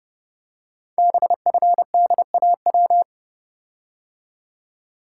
Morse code, heard as a keyed tone, about 30 wpm, 700 Hz. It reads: BFDAW
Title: Code Groups - 5 Letters - 30wpm